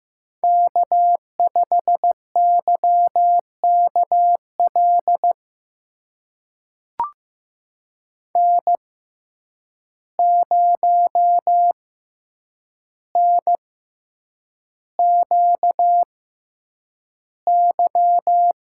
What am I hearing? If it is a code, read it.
Morse code, heard as a keyed tone, about 15 wpm, spaced as for 8 wpm, 700 Hz. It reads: K5YKL N0NQY
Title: USA Call Signs - 5 Characters - 6x Character Spacing - 15wpm